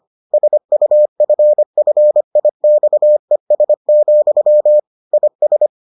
Morse code, heard as a keyed tone, about 25 wpm, 600 Hz. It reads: SUFFIXES, IS